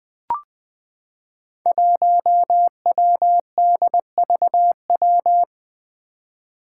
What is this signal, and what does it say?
Morse code, heard as a keyed tone, about 20 wpm, 700 Hz. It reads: 1WDVW
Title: Five Challenging Letter-Number Combinations 20wpm